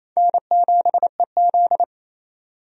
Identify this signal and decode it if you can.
Morse code, heard as a keyed tone, about 28 wpm, 700 Hz. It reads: N7EZ
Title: USA Call Signs - 4 Characters - 28wpm